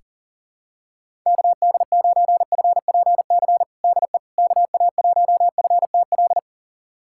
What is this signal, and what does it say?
Morse code, heard as a keyed tone, about 37 wpm, 700 Hz. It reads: KD9FPC DE KA1FTL
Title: USA Callsign DE Callsign - 6 Characters - 40wpm